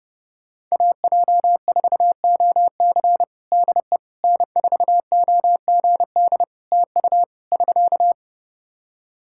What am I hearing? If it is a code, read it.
Morse code, heard as a keyed tone, about 31 wpm, 700 Hz. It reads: AJ4OC DE N4OGD TU <SK>
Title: Practice QSO - Exchange 4 - Ending - 30wpm